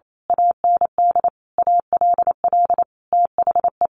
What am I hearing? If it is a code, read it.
Morse code, heard as a keyed tone, about 28 wpm, 700 Hz. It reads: AND ALL THE
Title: A Sound Of Thunder 28wpm